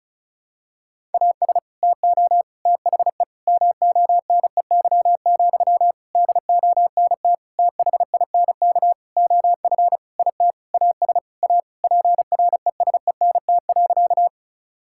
Morse code, heard as a keyed tone, about 36 wpm, 700 Hz. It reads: AS TO THE MONEY, DONT THINK OF IT AS A PRESENT.
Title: Sentences from Top 1000 Words - Review Part 5 - 35wpm